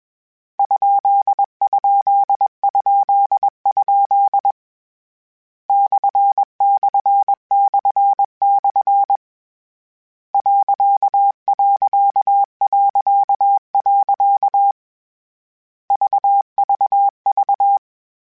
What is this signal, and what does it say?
Morse code, heard as a keyed tone, about 21 wpm, 800 Hz. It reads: ???? //// .... VVV
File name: Misc / Misc-mp3s / Punct @20